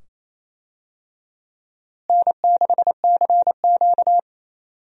Morse code, heard as a keyed tone, about 27 wpm, 700 Hz. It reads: N6CQ
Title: USA Call Signs - 4 Characters - Repeated 2x - 28wpm